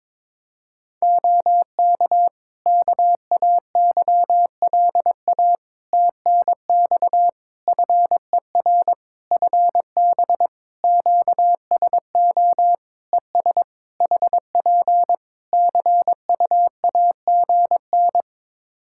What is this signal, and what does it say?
Morse code, heard as a keyed tone, about 22 wpm, 700 Hz. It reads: OK KAYLA TNX FER FB QSO ES HP CUAGN
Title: Practice QSO - Exchange 4 - Ending - 22wpm